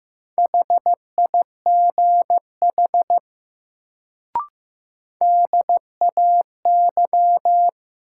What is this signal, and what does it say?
Morse code, heard as a keyed tone, about 16 wpm, 700 Hz. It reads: HIGH DAY
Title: Sentences from Top 200 Words - New Word Drill - 15wpm